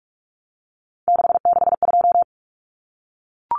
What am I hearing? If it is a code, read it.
Morse code, heard as a keyed tone, about 44 wpm, 700 Hz. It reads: 662
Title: Common Cut Numbers - 3-Digit 45wpm